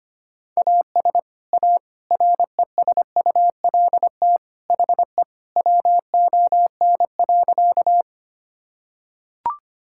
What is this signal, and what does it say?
Morse code, heard as a keyed tone, about 25 wpm, 700 Hz. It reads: AS A RESULT HE WON.
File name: practice / 25wpm / Sentences with 3-Word Phrases - 25wpm